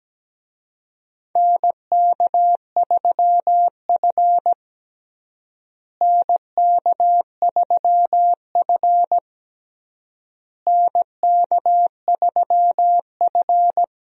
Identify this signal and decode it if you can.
Morse code, heard as a keyed tone, about 17 wpm, 700 Hz. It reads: NK3F NK3F NK3F
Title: USA Call Signs - 4 Characters - Repeated 3x - 17wpm